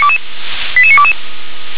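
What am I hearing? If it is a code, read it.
Five-tone selective call (ZVEI), heard as quick runs of five tones, tones of about 70 ms. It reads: 88922
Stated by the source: Five-tone ZVEI selective call 88083, received by SDR.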